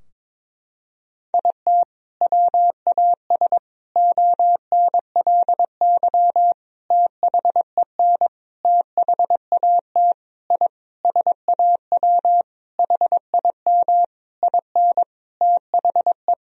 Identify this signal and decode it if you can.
Morse code, heard as a keyed tone, about 22 wpm, 700 Hz. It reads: IT WAS ONLY THEN THAT I SAW HIM IN THE
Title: Sentences from Top 1000 Words - Review Part 3 - 22wpm